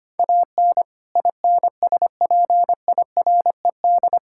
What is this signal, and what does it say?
Morse code, heard as a keyed tone, about 25 wpm, 700 Hz. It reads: AN INSPIRED
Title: Pangram Challenge v2 - 25wpm